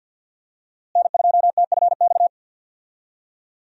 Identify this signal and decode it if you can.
Morse code, heard as a keyed tone, about 48 wpm, 700 Hz. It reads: N2TFX
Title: USA Call Signs - 5 Characters - Repeated 2x - 50wpm